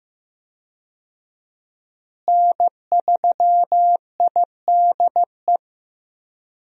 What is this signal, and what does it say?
Morse code, heard as a keyed tone, about 16 wpm, 700 Hz. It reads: N3IDE
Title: USA Call Signs - 5 Characters - Repeated 2x - 15wpm